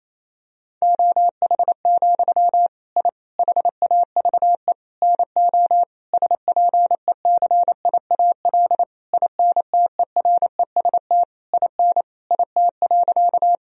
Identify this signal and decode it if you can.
Morse code, heard as a keyed tone, about 28 wpm, 700 Hz. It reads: OH, I HAVE NO SPECIAL INTEREST IN IT.
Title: Sentences from Top 1000 Words - Review Part 3 - 28wpm